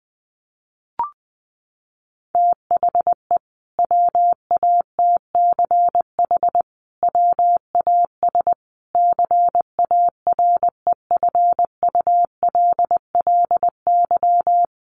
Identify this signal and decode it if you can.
Morse code, heard as a keyed tone, about 20 wpm, 700 Hz. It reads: THE WATCH WAS CAREFULLY
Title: Sentences from Top 1300 Words 20wpm